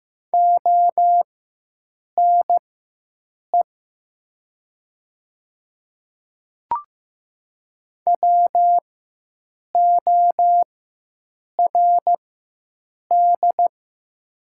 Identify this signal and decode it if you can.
Morse code, heard as a keyed tone, about 15 wpm, spaced as for 7 wpm, 700 Hz. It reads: ONE WORD
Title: 4x Character Spacing - Top 100 Words - 15wpm